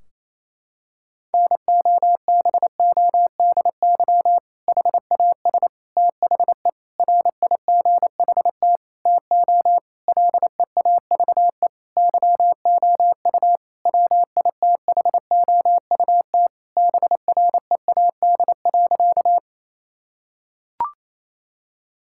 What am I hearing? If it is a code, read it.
Morse code, heard as a keyed tone, about 28 wpm, 700 Hz. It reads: NOBODY HAS THE RIGHT TO LEAVE YOU WITHOUT BREAD.
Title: Sentences from Top 1600 Words 28wpm